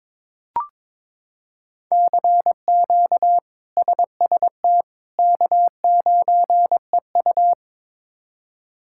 Morse code, heard as a keyed tone, about 21 wpm, 700 Hz. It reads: CQ SST K9EU
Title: K1USN SST QSO Practice - 22wpm